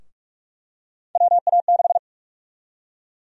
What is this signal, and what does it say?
Morse code, heard as a keyed tone, about 46 wpm, 700 Hz. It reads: WA6
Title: USA Call Sign Prefixes - 45wpm